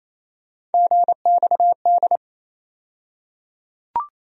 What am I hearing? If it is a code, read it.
Morse code, heard as a keyed tone, about 28 wpm, 700 Hz. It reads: GXD
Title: USA Call Sign Postfixes - 3 Letters 28wpm